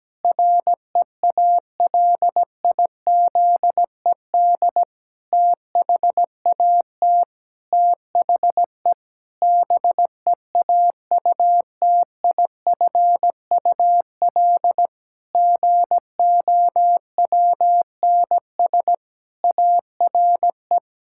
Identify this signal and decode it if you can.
Morse code, heard as a keyed tone, about 17 wpm, 700 Hz. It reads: REALIZED THAT THE BEAUTIFUL GOWNS ARE
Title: Pangram Challenge v2 - 17wpm